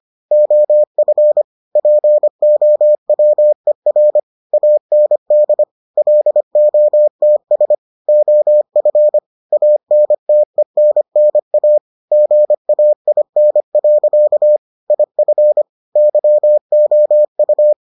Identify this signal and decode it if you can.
Morse code, heard as a keyed tone, about 25 wpm, 600 Hz. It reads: OF POWER AND LOTS OF ANTENNA GAIN. IF YOU